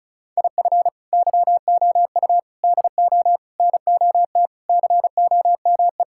Morse code, heard as a keyed tone, about 36 wpm, 700 Hz. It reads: IF YOU DO NOT COME
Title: Sentences from Top 1000 Words - Review Part 5 - 35wpm